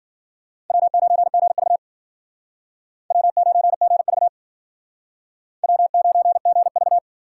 Morse code, heard as a keyed tone, about 51 wpm, 700 Hz. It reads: W9GV W9GV W9GV